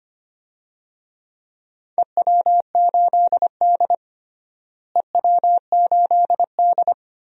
Morse code, heard as a keyed tone, about 25 wpm, 700 Hz. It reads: EW8D EW8D
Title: DXCC-100 Call Signs - 4 Characters - Repeated 2x - 25wpm